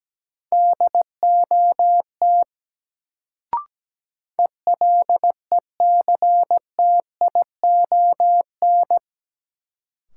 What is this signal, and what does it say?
Morse code, heard as a keyed tone, about 17 wpm, 700 Hz. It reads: DOT ELECTION
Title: Sentences from Top 1500 Words - New Word Drill Alt - 17wpm